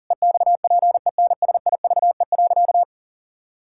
Morse code, heard as a keyed tone, about 40 wpm, 700 Hz. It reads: EXPENSIVE.
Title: Sentences from Top 1000 Words - Review Part 6 - 40wpm